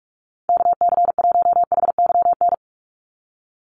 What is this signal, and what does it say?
Morse code, heard as a keyed tone, about 44 wpm, 700 Hz. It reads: KC1HYN
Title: USA Call Signs 45wpm